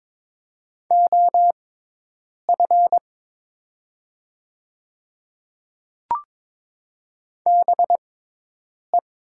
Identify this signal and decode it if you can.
Morse code, heard as a keyed tone, about 22 wpm, spaced as for 8 wpm, 700 Hz. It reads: OF BE
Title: 6x Character Spacing - Top 100 Words - 22wpm